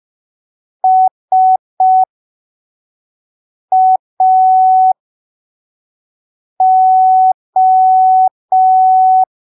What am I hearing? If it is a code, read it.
Morse code, heard as a keyed tone, about 5 wpm, spaced as for 3 wpm, 750 Hz. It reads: SAO
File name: arquivos / texto / texto 18 05ppm